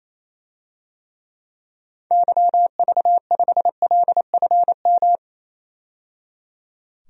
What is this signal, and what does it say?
Morse code, heard as a keyed tone, about 28 wpm, 700 Hz. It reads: YV5LFM
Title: DXCC-100 Call Signs - 6 Characters - 28wpm